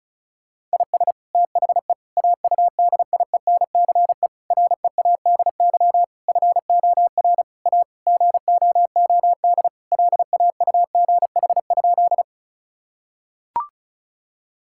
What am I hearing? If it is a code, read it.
Morse code, heard as a keyed tone, about 34 wpm, 700 Hz. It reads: IS THE AUDIENCE READY FOR A GOOD LAUGH?
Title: Sentences from Top 1000 Words - Review Part 5 - 35wpm